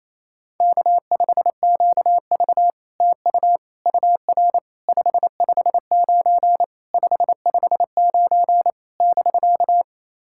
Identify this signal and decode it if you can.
Morse code, heard as a keyed tone, about 29 wpm, 660 Hz. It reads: K5QV TU UR 559 559 <BK>
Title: SOTA QSO Practice 28wpm